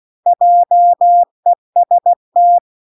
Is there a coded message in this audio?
Yes: Morse code from 0.3 to 2.6 s, heard as a keyed tone, about 16 wpm, 700 Hz. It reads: JEST